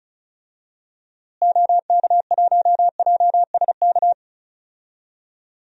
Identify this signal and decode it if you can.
Morse code, heard as a keyed tone, about 36 wpm, 700 Hz. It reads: OK1JSK
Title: DXCC-100 Call Signs - 6 Characters - Repeated 2x - 35wpm